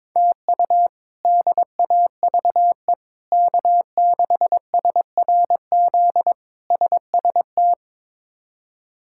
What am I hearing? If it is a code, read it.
Morse code, heard as a keyed tone, about 22 wpm, 700 Hz. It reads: TU DAVE K6SRZ SST
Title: K1USN SST QSO Practice - 22wpm